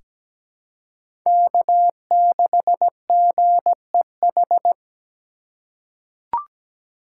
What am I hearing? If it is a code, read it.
Morse code, heard as a keyed tone, about 17 wpm, 700 Hz. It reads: K6GEH
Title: USA Call Signs - 3x Character Spacing - 17wpm